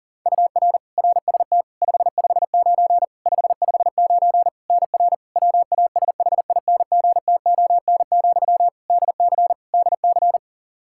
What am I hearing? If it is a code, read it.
Morse code, heard as a keyed tone, about 37 wpm, 700 Hz. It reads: UR RST 559 559 NR WASHINGTON, DC DC
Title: Practice QSO - Exchange 1 - Beginning - 40wpm